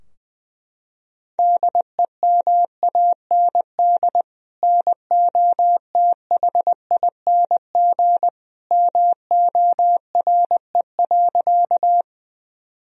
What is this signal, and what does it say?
Morse code, heard as a keyed tone, about 20 wpm, 700 Hz. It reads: DEMAND NOTHING MORE.
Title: Sentences from Top 1000 Words - Review Part 4 - 20wpm